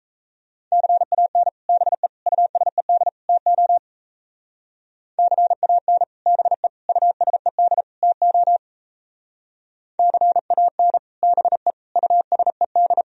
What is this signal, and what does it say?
Morse code, heard as a keyed tone, about 37 wpm, 700 Hz. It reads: CAN BE USED TO CAN BE USED TO CAN BE USED